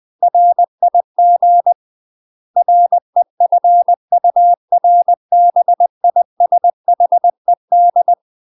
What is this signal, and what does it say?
Morse code, heard as a keyed tone, about 20 wpm, 700 Hz. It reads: RIG REFURBISHED
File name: UK MASTER QSOs TRAINING / 20WPM / UK MASTER QSOs 0023